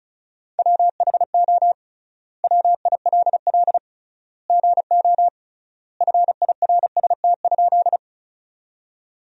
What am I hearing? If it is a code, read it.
Morse code, heard as a keyed tone, about 36 wpm, 700 Hz. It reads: WHO WILL GO FIRST?
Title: 3x Word-Spacing - Sentences from Top 100 Words - 35wpm